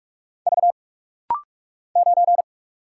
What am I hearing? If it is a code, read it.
Morse code, heard as a keyed tone, about 46 wpm, 700 Hz. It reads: V 9
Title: Single Letter-Number - Rapid-Fire 45wpm